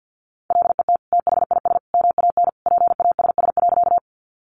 Morse code, heard as a keyed tone, about 50 wpm, 700 Hz. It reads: LET THIS MAN PASS.
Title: Sentences from Top 400 Words 50wpm